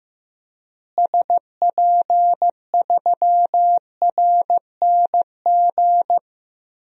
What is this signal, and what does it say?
Morse code, heard as a keyed tone, about 15 wpm, 700 Hz. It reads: SP3RNG
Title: DXCC-100 Call Signs - 6 Characters - Repeated 3x - 15wpm